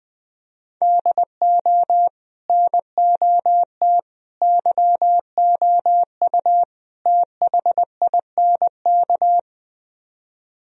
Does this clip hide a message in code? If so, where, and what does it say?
Yes: Morse code from 0.8 to 9.4 s, heard as a keyed tone, about 21 wpm, 700 Hz. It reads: DO NOT YOU THINK